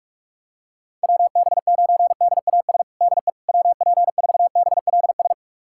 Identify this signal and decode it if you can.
Morse code, heard as a keyed tone, about 46 wpm, 700 Hz. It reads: WB9DAS DE WP4BLS
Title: USA Callsign DE Callsign - 6 Characters - 45wpm